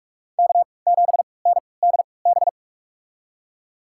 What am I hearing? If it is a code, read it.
Morse code, heard as a keyed tone, about 46 wpm, spaced as for 25 wpm, 700 Hz. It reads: K7NDB
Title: USA Call Signs - 5 Characters - 3x Character Spacing - 45wpm